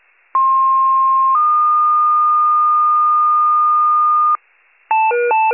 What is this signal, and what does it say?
Two-tone page, heard as a one-second tone, then a three-second tone, about 1060 then 1220 Hz, with noise. It is heard from 0.3 to 4.4 s.